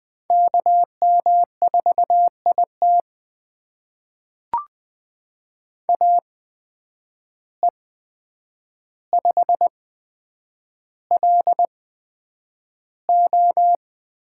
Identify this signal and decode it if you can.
Morse code, heard as a keyed tone, about 20 wpm, spaced as for 9 wpm, 700 Hz. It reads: KM4IT AE5LO
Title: USA Call Signs - 5 Characters - 8x Character Spacing - 20wpm